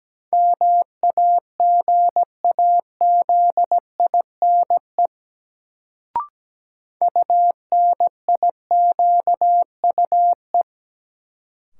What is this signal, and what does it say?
Morse code, heard as a keyed tone, about 17 wpm, 700 Hz. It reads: MAGAZINE UNIQUE
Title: Sentences from Top 1000 Words - Review Part 7 - New Word Drill - 17wpm